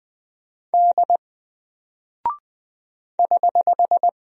Morse code, heard as a keyed tone, about 20 wpm, 700 Hz. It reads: D <HH>
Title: Single Character (Letter:Number:Punctuation:Prosign) 20wpm